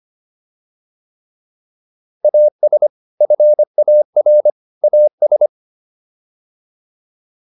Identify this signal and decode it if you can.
Morse code, heard as a keyed tone, about 25 wpm, 600 Hz. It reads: AS FAR AS